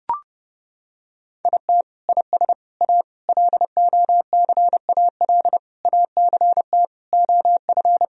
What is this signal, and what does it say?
Morse code, heard as a keyed tone, about 30 wpm, 700 Hz. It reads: IT IS A LOCAL ACT OF
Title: Sentences from Top 1000 Words - Review Part 3 - 30wpm